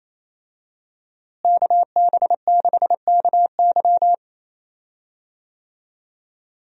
Morse code, heard as a keyed tone, about 28 wpm, 700 Hz. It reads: KB6KY
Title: USA Call Signs - 5 Characters - Repeated 3x - 28wpm